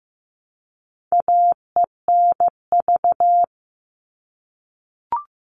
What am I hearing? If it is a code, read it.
Morse code, heard as a keyed tone, about 15 wpm, 700 Hz. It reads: AENV
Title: All Cut Numbers - 4-Digit 15wpm